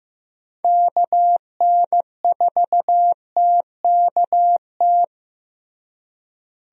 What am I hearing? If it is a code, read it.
Morse code, heard as a keyed tone, about 15 wpm, 700 Hz. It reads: KN4TKT
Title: USA Call Signs - 6 Characters - Repeated 3x - 15wpm